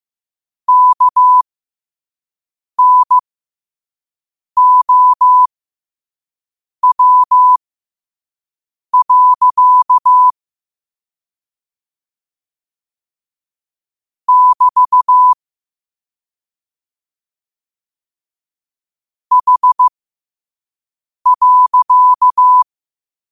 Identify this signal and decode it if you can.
Morse code, heard as a keyed tone, about 15 wpm, spaced as for 5 wpm, 1000 Hz. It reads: KNOW. <BT> H.
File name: 5wpm